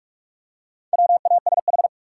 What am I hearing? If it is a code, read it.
Morse code, heard as a keyed tone, about 44 wpm, 700 Hz. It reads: WASH
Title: Four Letter Words - 45wpm